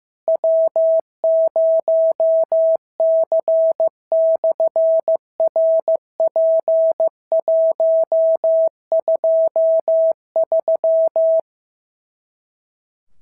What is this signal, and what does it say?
Morse code, heard as a keyed tone, about 15 wpm, 650 Hz. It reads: W0C/RP123
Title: SOTA QSO Practice 15wpm